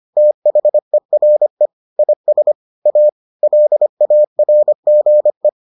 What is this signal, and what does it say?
Morse code, heard as a keyed tone, about 25 wpm, 600 Hz. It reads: THERE IS A LARGE